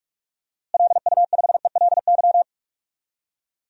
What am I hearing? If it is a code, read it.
Morse code, heard as a keyed tone, about 41 wpm, 700 Hz. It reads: LU5ELY
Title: DXCC-100 Call Signs - 6 Characters - Repeated 2x - 45wpm